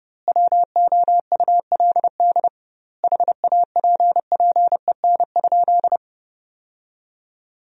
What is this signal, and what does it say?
Morse code, heard as a keyed tone, about 30 wpm, 700 Hz. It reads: WOULD HAPPEN?
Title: 2x Word-Spacing - Sentences from Top 400 Words - 30wpm